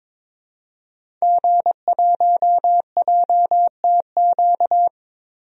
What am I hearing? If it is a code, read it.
Morse code, heard as a keyed tone, about 22 wpm, 700 Hz. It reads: G1JTQ
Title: DXCC-100 Call Signs - 5 Characters - Repeated 3x - 22wpm